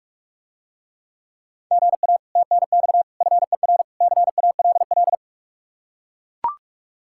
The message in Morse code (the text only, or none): GA TNX FER CALL